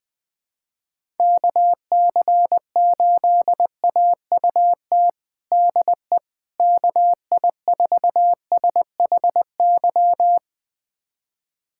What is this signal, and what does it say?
Morse code, heard as a keyed tone, about 21 wpm, 700 Hz. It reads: KC8AUT DE KI4SHY
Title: USA Callsign DE Callsign - 6 Characters - 20wpm